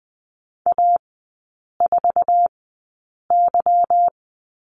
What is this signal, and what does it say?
Morse code, heard as a keyed tone, about 20 wpm, spaced as for 8 wpm, 690 Hz. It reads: A4Y
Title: 212 Lesson 34e - Number 8 - Three Characters 20wpm